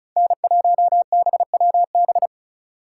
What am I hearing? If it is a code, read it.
Morse code, heard as a keyed tone, about 36 wpm, 700 Hz. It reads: N1BWB
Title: USA Call Signs - 4x Character Spacing - 35wpm